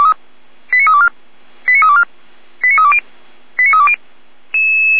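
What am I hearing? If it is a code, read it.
Five-tone selective call (ZVEI), heard as quick runs of five tones, tones of about 70 ms. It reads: 89324 89324 89320 89320
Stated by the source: ZVEI five-tone selective call received with an SDR.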